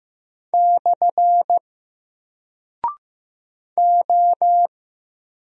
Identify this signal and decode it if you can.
Morse code, heard as a keyed tone, about 15 wpm, 700 Hz. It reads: / O
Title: Single Character (Letter:Number:Punctuation:Prosign) 15wpm